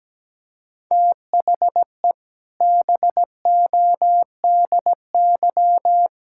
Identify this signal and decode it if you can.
Morse code, heard as a keyed tone, about 17 wpm, 700 Hz. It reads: THE BODY
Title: Sentences from Top 1000 Words - Review Part 5 - 17wpm